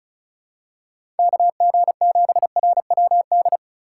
Morse code, heard as a keyed tone, about 36 wpm, 700 Hz. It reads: KG7RWD
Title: USA Call Signs - 6 Characters - 4x Character Spacing - 35wpm